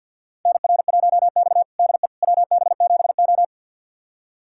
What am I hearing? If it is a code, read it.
Morse code, heard as a keyed tone, about 46 wpm, 700 Hz. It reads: NR1X DE WB7O
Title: USA Callsign DE Callsign - 4 Characters - 50wpm